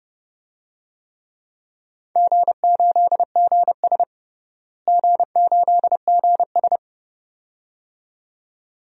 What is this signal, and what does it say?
Morse code, heard as a keyed tone, about 30 wpm, 700 Hz. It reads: G8GS G8GS
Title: DXCC-100 Call Signs - 4 Characters - Repeated 2x - 30wpm